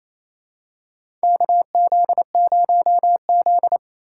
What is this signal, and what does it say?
Morse code, heard as a keyed tone, about 28 wpm, 700 Hz. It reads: KZ0Z
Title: USA Call Signs - 4 Characters - 2x Character Spacing - 28wpm